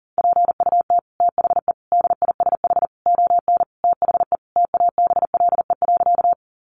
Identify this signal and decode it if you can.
Morse code, heard as a keyed tone, about 37 wpm, 700 Hz. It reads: PUT THE DISH ON THE TABLE.
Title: Sentences from Top 1300 Words 40wpm